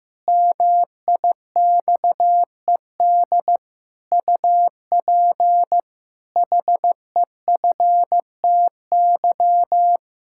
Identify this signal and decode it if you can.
Morse code, heard as a keyed tone, about 15 wpm, 700 Hz. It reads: MIXED UP HEFTY
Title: Pangram Challenge v2 - 15wpm